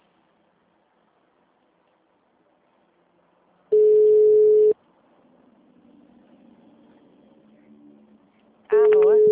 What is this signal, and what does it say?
Ringback tone (Europe, 425 Hz), with speech.
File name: out- -2011- - - .3582